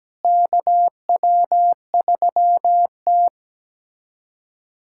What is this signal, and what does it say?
Morse code, heard as a keyed tone, about 17 wpm, 700 Hz. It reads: KW3T
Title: USA Callsign DE Callsign - 4 Characters - 17wpm